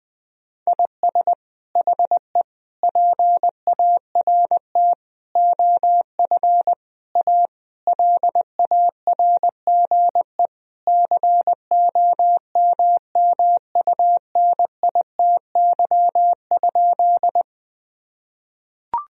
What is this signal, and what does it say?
Morse code, heard as a keyed tone, about 21 wpm, 700 Hz. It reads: IS HE PART OF A LARGE COMMUNITY?
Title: Sentences from Top 1000 Words - Review Part 4 - 20wpm